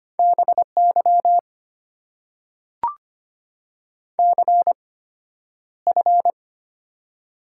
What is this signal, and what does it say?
Morse code, heard as a keyed tone, about 25 wpm, spaced as for 11 wpm, 700 Hz. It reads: BY CF
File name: Top 90 QSO Elements - 8x Character Spacing - 25wpm